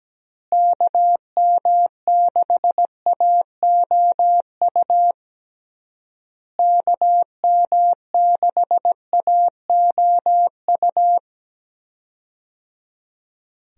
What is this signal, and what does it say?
Morse code, heard as a keyed tone, about 17 wpm, 700 Hz. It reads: KM6AOU KM6AOU
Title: USA Call Signs - 6 Characters - Repeated 3x - 17wpm